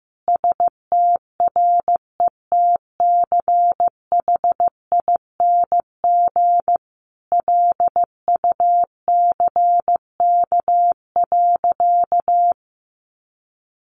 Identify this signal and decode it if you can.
Morse code, heard as a keyed tone, about 15 wpm, 700 Hz. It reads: STRETCHING LUCK.
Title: A Sound Of Thunder 15wpm